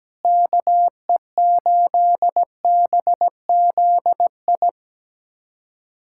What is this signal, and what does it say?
Morse code, heard as a keyed tone, about 17 wpm, 700 Hz. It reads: KE8BZI
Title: USA Callsign DE Callsign 17wpm